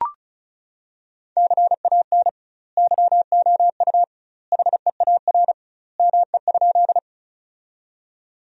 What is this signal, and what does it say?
Morse code, heard as a keyed tone, about 36 wpm, 700 Hz. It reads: CAN YOU HEAR ME?
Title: 2x Word-Spacing - Sentences from Top 300 Words - 35wpm